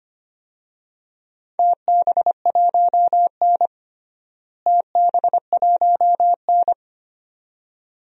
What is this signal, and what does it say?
Morse code, heard as a keyed tone, about 25 wpm, 700 Hz. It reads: TB1N TB1N
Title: DXCC-100 Call Signs - 4 Characters - Repeated 2x - 25wpm